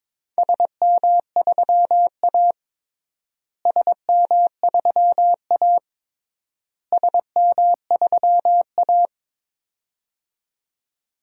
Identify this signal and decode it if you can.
Morse code, heard as a keyed tone, about 22 wpm, 700 Hz. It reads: SM3A SM3A SM3A
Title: DXCC-100 Call Signs - 4 Characters - Repeated 3x - 22wpm